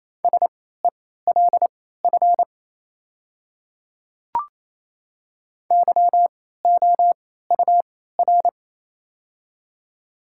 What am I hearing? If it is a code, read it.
Morse code, heard as a keyed tone, about 28 wpm, spaced as for 16 wpm, 700 Hz. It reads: SELF YOUR
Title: 3x Character Spacing - Top 300 Words - 28wpm